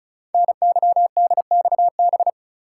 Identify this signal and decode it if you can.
Morse code, heard as a keyed tone, about 36 wpm, 700 Hz. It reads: NYDXB
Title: Code Groups - 5 Letters - 35wpm